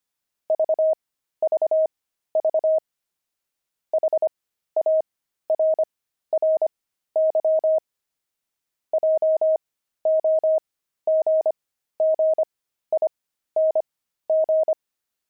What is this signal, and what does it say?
Morse code, heard as a keyed tone, about 25 wpm, spaced as for 13 wpm, 630 Hz. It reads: VVV HARRY JOGGING
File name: pangram-fs-13-cs-25-monks